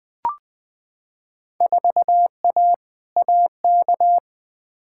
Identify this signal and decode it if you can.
Morse code, heard as a keyed tone, about 21 wpm, 700 Hz. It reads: 4A AK
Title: Field Day Exchanges - 20wpm